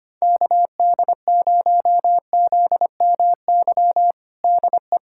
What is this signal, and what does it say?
Morse code, heard as a keyed tone, about 25 wpm, 700 Hz. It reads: KD0ZMY DE
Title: USA Callsign DE Callsign - 6 Characters - 25wpm